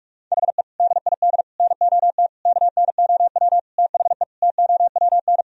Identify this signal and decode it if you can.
Morse code, heard as a keyed tone, about 42 wpm, 700 Hz. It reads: HE DID NOT KNOW THE TOWN
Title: Sentences from Top 1000 Words - Review Part 2 - 45wpm